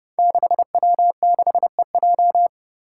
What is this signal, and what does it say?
Morse code, heard as a keyed tone, about 30 wpm, 700 Hz. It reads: 6W6EJ
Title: Five Challenging Letter-Number Combinations 30wpm